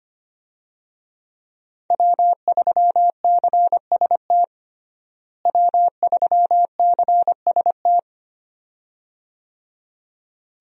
Morse code, heard as a keyed tone, about 25 wpm, 700 Hz. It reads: W3CST W3CST
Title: USA Call Signs - 5 Characters - Repeated 2x - 25wpm